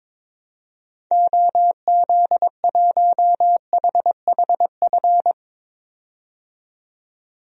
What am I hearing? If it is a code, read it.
Morse code, heard as a keyed tone, about 22 wpm, 700 Hz. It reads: OZ1HHF
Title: DXCC-100 Call Signs - 6 Characters - 22wpm